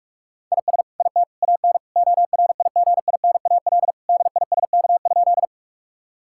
Morse code, heard as a keyed tone, about 41 wpm, 700 Hz. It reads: IS IT AN ORIGINAL DISK?
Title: Sentences from Top 1000 Words - Review Part 6 - 45wpm